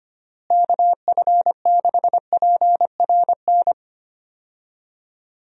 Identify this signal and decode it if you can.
Morse code, heard as a keyed tone, about 25 wpm, 700 Hz. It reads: KF6PRN
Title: USA Call Signs - 6 Characters - Repeated 2x - 25wpm